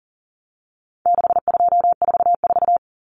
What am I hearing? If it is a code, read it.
Morse code, heard as a keyed tone, about 40 wpm, 700 Hz. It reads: 6244
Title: Numbers - 4-Digit 40wpm